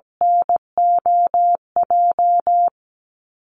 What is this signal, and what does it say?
Morse code, heard as a keyed tone, about 17 wpm, 700 Hz. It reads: NOJ
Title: Call Signs Repeated 2x - 17wpm